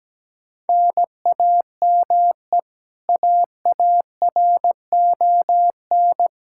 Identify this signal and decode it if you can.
Morse code, heard as a keyed tone, about 17 wpm, 700 Hz. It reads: NAME AARON
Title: Practice QSO - Exchange 1 - Beginning - 17wpm